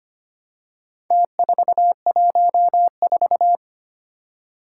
Morse code, heard as a keyed tone, about 25 wpm, 700 Hz. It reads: T414
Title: Four Challenging Letter-Number Combinations 25wpm